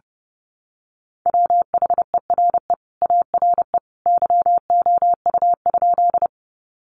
Morse code, heard as a keyed tone, about 28 wpm, 700 Hz. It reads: WHERE ARE YOU?
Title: Sentences from Top 200 Words 30wpm